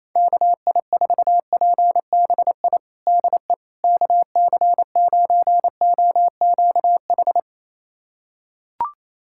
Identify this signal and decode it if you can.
Morse code, heard as a keyed tone, about 28 wpm, 700 Hz. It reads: KI4PBI DE KC9OQH
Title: USA Callsign DE Callsign 28wpm